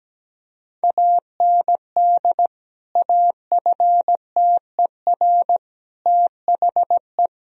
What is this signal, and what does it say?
Morse code, heard as a keyed tone, about 17 wpm, 700 Hz. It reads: AND AFTER THE
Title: Sentences from Top 1000 Words - Review Part 2 - 17wpm